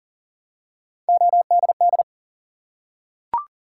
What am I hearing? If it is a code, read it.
Morse code, heard as a keyed tone, about 40 wpm, 700 Hz. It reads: ODD
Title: Three Letter Words - 40wpm